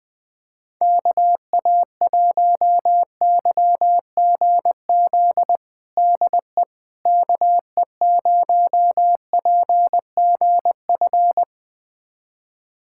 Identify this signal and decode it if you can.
Morse code, heard as a keyed tone, about 20 wpm, 700 Hz. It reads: KA1YGZ DE KE0PGF
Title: USA Callsign DE Callsign - 6 Characters - 20wpm